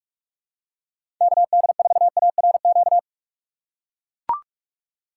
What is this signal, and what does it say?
Morse code, heard as a keyed tone, about 41 wpm, 700 Hz. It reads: KD4ARQ
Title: USA Call Signs - 6 Characters - 3x Character Spacing - 45wpm